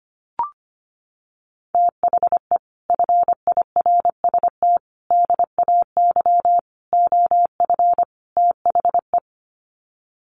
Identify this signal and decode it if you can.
Morse code, heard as a keyed tone, about 25 wpm, 700 Hz. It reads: THE FIRST DAY OF THE
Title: Sets of 5 Words Encore2 25wpm